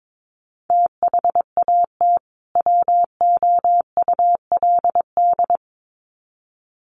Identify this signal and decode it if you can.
Morse code, heard as a keyed tone, about 22 wpm, 700 Hz. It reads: THAT WOULD
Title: Sets of 2 Words 22wpm